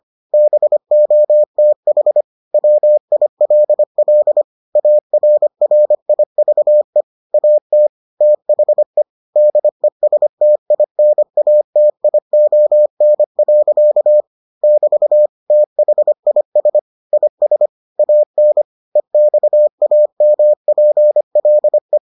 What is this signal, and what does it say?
Morse code, heard as a keyed tone, about 25 wpm, 600 Hz. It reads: BOTH WILL ARRIVE AT THE DESTINATION. <BT> THIS IS AN EXAMPLE